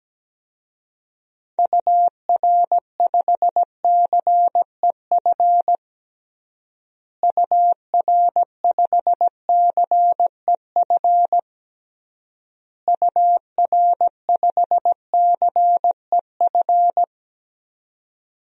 Morse code, heard as a keyed tone, about 17 wpm, 700 Hz. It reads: UR5CEF UR5CEF UR5CEF
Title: DXCC-100 Call Signs - 6 Characters - Repeated 3x - 17wpm